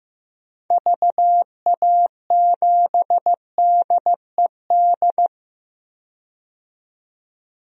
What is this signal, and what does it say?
Morse code, heard as a keyed tone, about 15 wpm, 700 Hz. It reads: VA7DED
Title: DXCC-100 Call Signs - 6 Characters - 15wpm